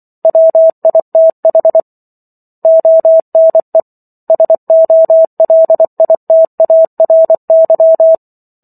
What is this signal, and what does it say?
Morse code, heard as a keyed tone, about 24 wpm, 650 Hz. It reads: WITH ONE SOLITARY